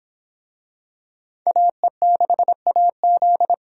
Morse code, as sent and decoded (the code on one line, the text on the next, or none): .- . -.... .- --..
AE6AZ